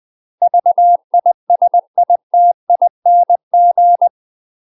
Morse code, heard as a keyed tone, about 20 wpm, 700 Hz. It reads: VISITING
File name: UK MASTER QSOs TRAINING / 20WPM / UK MASTER QSOs 0042